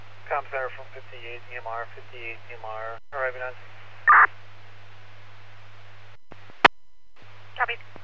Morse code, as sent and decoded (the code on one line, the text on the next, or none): none